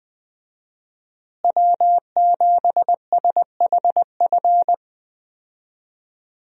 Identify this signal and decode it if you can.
Morse code, heard as a keyed tone, about 20 wpm, 700 Hz. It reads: W7SHF
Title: USA Call Signs - 5 Characters - Repeated 2x - 20wpm